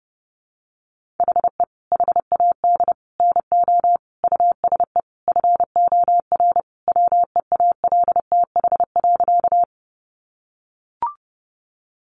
Morse code, heard as a keyed tone, about 30 wpm, 700 Hz. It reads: HE HAD NO USE FOR WEALTH.
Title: Sentences from Top 1200 Words 30wpm